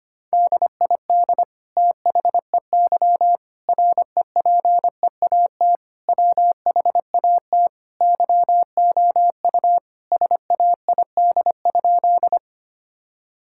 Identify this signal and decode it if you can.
Morse code, heard as a keyed tone, about 25 wpm, 700 Hz. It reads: DID THEY REPEAT WHAT YOU SAID?
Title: Sentences from Top 1600 Words 25wpm